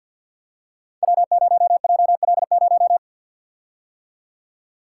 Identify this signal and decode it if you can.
Morse code, heard as a keyed tone, about 46 wpm, 700 Hz. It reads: W0JL0
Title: Code Groups - 5 Letter:Numbers - 50wpm